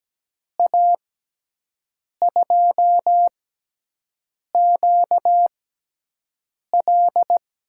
Morse code, heard as a keyed tone, about 17 wpm, spaced as for 6 wpm, 700 Hz. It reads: A2QL
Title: USA Call Signs - 5 Characters - 6x Character Spacing - 17wpm